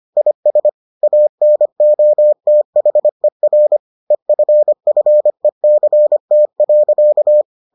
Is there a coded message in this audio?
Yes: Morse code from 0.2 to 7.4 s, heard as a keyed tone, about 25 wpm, 600 Hz. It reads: IS ANOTHER EFFECT.